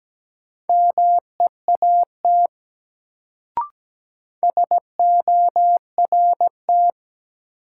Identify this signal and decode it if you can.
Morse code, heard as a keyed tone, about 17 wpm, 700 Hz. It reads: MEAT SORT
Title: Sentences from Top 1000 Words - Review Part 3 - New Word Drill - 17wpm